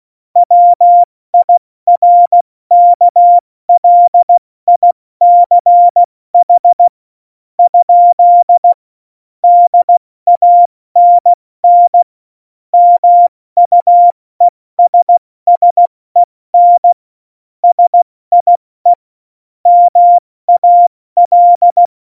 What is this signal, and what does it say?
Morse code, heard as a keyed tone, about 16 wpm, spaced as for 14 wpm, 700 Hz. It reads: WIRKLICH ? DANN MUESSEN SIE MAL
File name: Morsekurs / Witze / Witz 06 March 2018, 70 F80